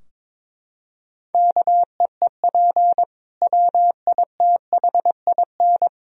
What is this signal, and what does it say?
Morse code, heard as a keyed tone, about 22 wpm, 700 Hz. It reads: KEEP WITHIN
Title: Sentences from Top 1000 Words - Review Part 4 - 22wpm